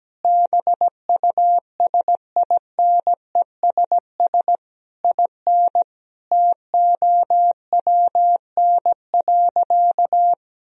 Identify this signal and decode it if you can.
Morse code, heard as a keyed tone, about 17 wpm, 700 Hz. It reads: BUSINESS IN TOWN.
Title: Sentences from Top 1000 Words - Review Part 2 - 17wpm